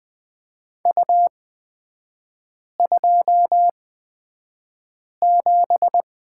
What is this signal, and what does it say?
Morse code, heard as a keyed tone, about 20 wpm, spaced as for 5 wpm, 690 Hz. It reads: U27
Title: 229 Lesson 37e - Letter x - Three Characters 20wpm